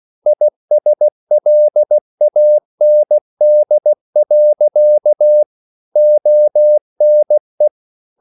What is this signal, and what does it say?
Morse code, heard as a keyed tone, about 16 wpm, 600 Hz. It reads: ISLAND. ONE
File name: CW / Practice / EnglishSayings16